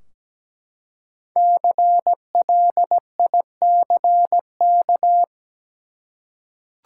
Morse code, heard as a keyed tone, about 17 wpm, 700 Hz. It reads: CLICK
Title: Five Letter Words - 17wpm